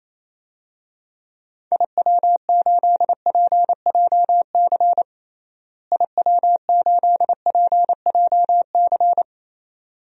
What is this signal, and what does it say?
Morse code, heard as a keyed tone, about 28 wpm, 700 Hz. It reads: IW8PJC IW8PJC
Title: DXCC-100 Call Signs - 6 Characters - Repeated 2x - 28wpm